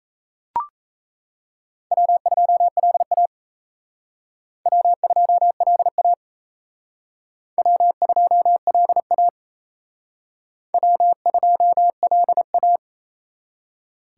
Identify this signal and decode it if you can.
Morse code, heard as a keyed tone, about 35 wpm, 700 Hz. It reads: W2LA W2LA W2LA W2LA
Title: Speed Racing - USA 1x2 Call Signs 28wpm